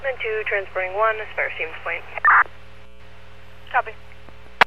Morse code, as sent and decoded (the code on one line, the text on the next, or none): none